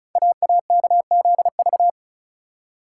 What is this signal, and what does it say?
Morse code, heard as a keyed tone, about 36 wpm, 700 Hz. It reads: AAKZV
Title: Code Groups - 5 Letter:Numbers - 35wpm